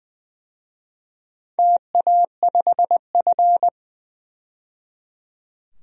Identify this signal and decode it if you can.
Morse code, heard as a keyed tone, about 20 wpm, 700 Hz. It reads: TA5F
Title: DXCC-100 Call Signs - 4 Characters - 20wpm